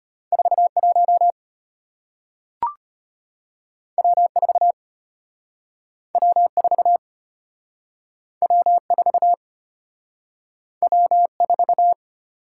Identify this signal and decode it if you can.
Morse code, heard as a keyed tone, about 33 wpm, 690 Hz. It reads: 41 W4 W4 W4 W4